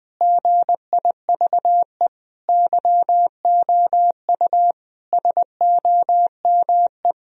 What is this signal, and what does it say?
Morse code, heard as a keyed tone, about 20 wpm, 700 Hz. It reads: GIVE YOU SOME
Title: Sentences from Top 1000 Words - Review Part 3 - 20wpm